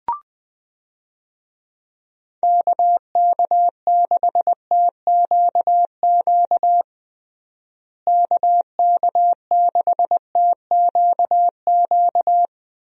Morse code, heard as a keyed tone, about 20 wpm, 700 Hz. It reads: KK6TQQ KK6TQQ
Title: USA Call Signs - 6 Characters - Repeated 3x - 20wpm